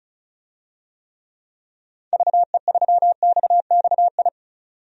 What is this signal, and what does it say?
Morse code, heard as a keyed tone, about 36 wpm, 700 Hz. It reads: VE3XXI
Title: DXCC-100 Call Signs - 6 Characters - 35wpm